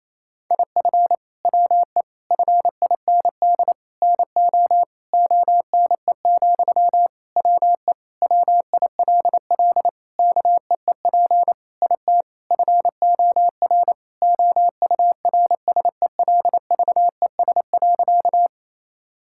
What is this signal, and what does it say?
Morse code, heard as a keyed tone, about 28 wpm, 700 Hz. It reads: IF WE FIND NO ONE, WE WILL KEEP IT FOR OURSELVES.
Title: Sentences from Top 1600 Words 28wpm